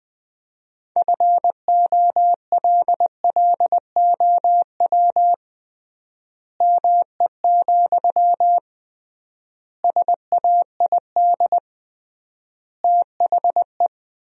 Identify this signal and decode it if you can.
Morse code, heard as a keyed tone, about 21 wpm, 700 Hz. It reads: FOLLOW ME, SAID THE
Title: 3x Word-Spacing - Sentences from Top 200 Words - 20wpm